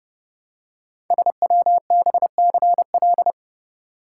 Morse code, heard as a keyed tone, about 29 wpm, 700 Hz. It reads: SWBCL
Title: Code Groups - 5 Letters - 30wpm